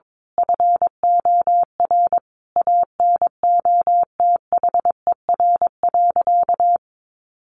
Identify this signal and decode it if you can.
Morse code, heard as a keyed tone, about 22 wpm, 700 Hz. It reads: FOR ANOTHER.
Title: Sentences from Top 1400 Words 22wpm